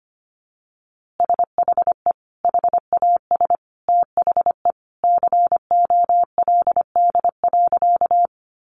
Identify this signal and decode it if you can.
Morse code, heard as a keyed tone, about 25 wpm, 700 Hz. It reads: SHE HAS THE COLD.
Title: Easy Phrases 25wpm